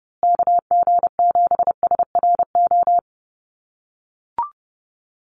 Morse code, heard as a keyed tone, about 30 wpm, 700 Hz. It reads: KG7SRO
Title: USA Call Signs Repeated 3x 30wpm